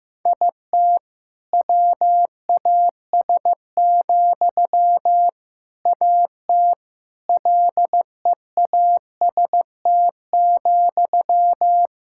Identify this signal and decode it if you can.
Morse code, heard as a keyed tone, about 15 wpm, 700 Hz. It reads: IT WAS, AT LEAST,
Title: Sentences from Top 1000 Words - Review Part 6 - 15wpm